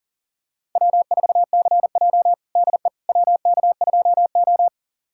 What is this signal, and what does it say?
Morse code, heard as a keyed tone, about 40 wpm, 700 Hz. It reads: W4CJ DE WK2O
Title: USA Callsign DE Callsign - 4 Characters - 40wpm